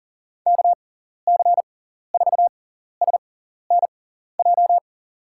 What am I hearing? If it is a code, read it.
Morse code, heard as a keyed tone, about 40 wpm, spaced as for 14 wpm, 700 Hz. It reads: KC4SNJ
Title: USA Call Signs - 6 Characters - 6x Character Spacing - 40wpm